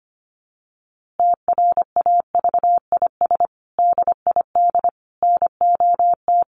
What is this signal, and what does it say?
Morse code, heard as a keyed tone, about 25 wpm, 700 Hz. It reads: TRAVIS DID NOT
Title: A Sound Of Thunder 25wpm